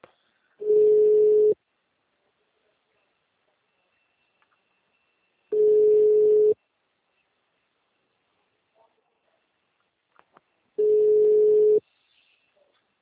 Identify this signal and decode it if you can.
Ringback tone (Europe, 425 Hz).